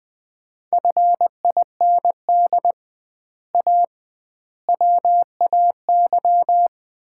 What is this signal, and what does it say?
Morse code, heard as a keyed tone, about 20 wpm, 700 Hz. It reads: FIND A WAY
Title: 2x Word-Spacing - Sets of 4 Words - 20wpm